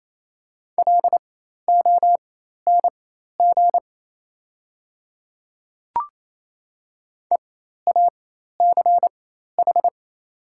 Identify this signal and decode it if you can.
Morse code, heard as a keyed tone, about 28 wpm, spaced as for 13 wpm, 700 Hz. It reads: LONG EACH
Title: 4x Character Spacing - Top 100 Words - 28wpm